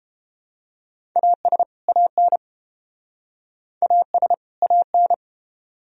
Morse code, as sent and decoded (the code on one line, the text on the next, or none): .- ... / .- -. / .- ... / .- -.
AS AN AS AN